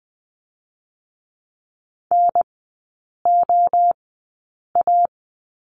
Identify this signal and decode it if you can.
Morse code, heard as a keyed tone, about 20 wpm, spaced as for 8 wpm, 700 Hz. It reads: NOA
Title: 074 Lesson 13e - Letter L - Three Characters 20wpm